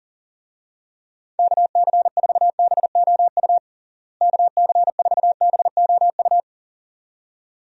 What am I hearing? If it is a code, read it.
Morse code, heard as a keyed tone, about 37 wpm, 700 Hz. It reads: KC4BOU KC4BOU
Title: USA Call Signs - 6 Characters - Repeated 2x - 40wpm